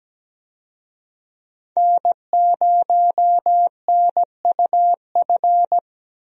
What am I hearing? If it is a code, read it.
Morse code, heard as a keyed tone, about 17 wpm, 700 Hz. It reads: N0NUF
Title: USA Call Signs - 5 Characters - 17wpm